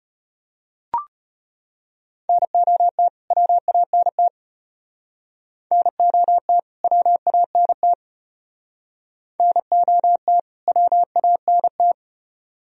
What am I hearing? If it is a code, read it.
Morse code, heard as a keyed tone, about 33 wpm, 700 Hz. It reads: NOT WANT NOT WANT NOT WANT